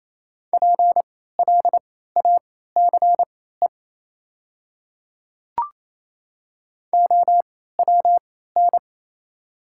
Morse code, heard as a keyed tone, about 28 wpm, spaced as for 16 wpm, 700 Hz. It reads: PLACE OWN
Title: 3x Character Spacing - Top 300 Words - 28wpm